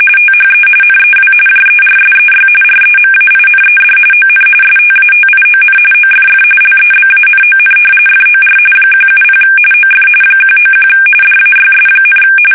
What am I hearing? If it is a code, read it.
RTTY, about 45.45 baud. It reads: THE NEW JERSEY REMAINED ON STATION OFF LEBANON FOR 170 OF 191 DAYS,
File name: fsk850 / battleshipnj